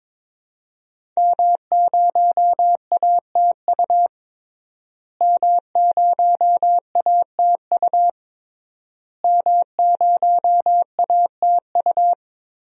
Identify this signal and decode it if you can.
Morse code, heard as a keyed tone, about 22 wpm, 700 Hz. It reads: M0ATU M0ATU M0ATU
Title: DXCC-100 Call Signs - 5 Characters - Repeated 3x - 22wpm